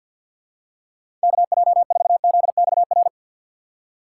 Morse code, heard as a keyed tone, about 51 wpm, 690 Hz. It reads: KJ4ZXR
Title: USA Call Signs - 6 Characters - 3x Character Spacing - 50wpm